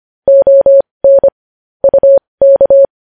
Morse code, heard as a keyed tone, about 25 wpm, spaced as for 20 wpm, 550 Hz. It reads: ON UK
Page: news headlines in morse code at 20 wpm